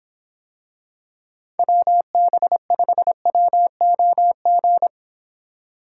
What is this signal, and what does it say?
Morse code, heard as a keyed tone, about 26 wpm, 700 Hz. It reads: WB5WOG